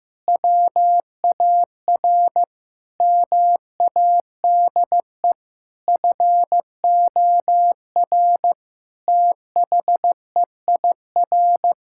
Morse code, heard as a keyed tone, about 15 wpm, 700 Hz. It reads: WAR MADE FOR THEIR
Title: Sentences from Top 1000 Words - Review Part 6 - 15wpm